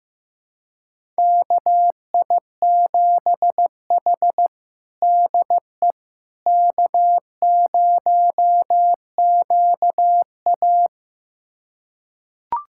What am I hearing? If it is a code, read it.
Morse code, heard as a keyed tone, about 15 wpm, 700 Hz. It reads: KI7H DE K0QA
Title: USA Callsign DE Callsign - 4 Characters - 15wpm